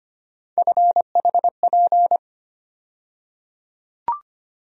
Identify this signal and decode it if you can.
Morse code, heard as a keyed tone, about 25 wpm, 700 Hz. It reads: FHP
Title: USA Call Sign Postfixes - 3 Letters 25wpm